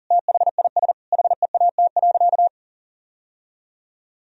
Morse code, heard as a keyed tone, about 37 wpm, 700 Hz. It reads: THIS HEAT.
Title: Sentences from Top 1000 Words - Review Part 3 - 40wpm